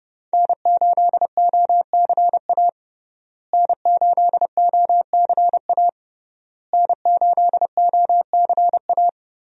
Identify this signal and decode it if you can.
Morse code, heard as a keyed tone, about 30 wpm, 700 Hz. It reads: N8OCA N8OCA N8OCA
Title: USA Call Signs - 5 Characters - Repeated 3x - 30wpm